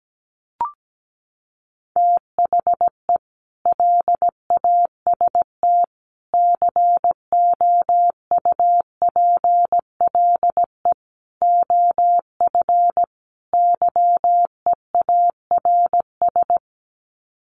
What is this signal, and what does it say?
Morse code, heard as a keyed tone, about 17 wpm, 700 Hz. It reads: THE LAST COUPLE OF YEARS
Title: Sets of 5 Words 17wpm